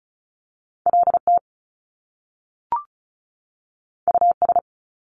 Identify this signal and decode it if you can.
Morse code, heard as a keyed tone, about 36 wpm, 700 Hz. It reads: LT US
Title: Top Two Letter Combinations in Words 35wpm